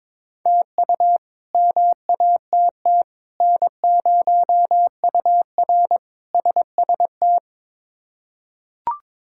Morse code, heard as a keyed tone, about 22 wpm, 700 Hz. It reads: TU MATT N0UR SST
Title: K1USN SST QSO Practice - 22wpm